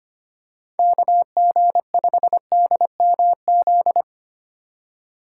Morse code, heard as a keyed tone, about 25 wpm, 700 Hz. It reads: KG5DMZ
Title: USA Call Signs - 6 Characters - 2x Character Spacing - 25wpm